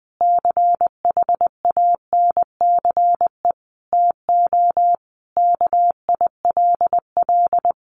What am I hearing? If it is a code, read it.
Morse code, heard as a keyed tone, about 20 wpm, 700 Hz. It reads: CHANCE TO KILL
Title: Sentences from Top 1400 Words 20wpm